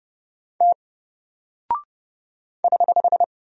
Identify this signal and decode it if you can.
Morse code, heard as a keyed tone, about 30 wpm, 700 Hz. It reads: T <HH>
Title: Single Character (Letter:Number:Punctuation:Prosign) Alt 30wpm